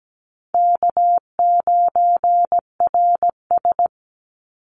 Morse code, heard as a keyed tone, about 17 wpm, 700 Hz. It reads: K9RS
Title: Easy Call Signs - 17wpm